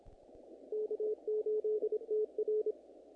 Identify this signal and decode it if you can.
Morse code, heard as a keyed tone, about 27 wpm, 470 Hz. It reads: K8TR
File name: contests / naqp-cw-k5tr-op-k5pi- -1900